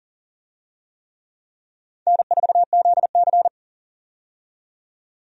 Morse code, heard as a keyed tone, about 40 wpm, 700 Hz. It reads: N4ZC
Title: USA Call Signs - 4 Characters - 40wpm